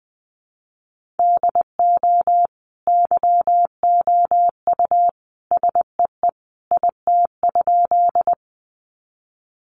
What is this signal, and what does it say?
Morse code, heard as a keyed tone, about 20 wpm, 700 Hz. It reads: DO YOU SEE IT?
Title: Sentences from Top 100 Words 20wpm